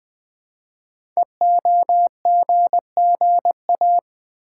Morse code, heard as a keyed tone, about 20 wpm, 700 Hz. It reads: EOGGA
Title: Code Groups - 5 Letters - 20wpm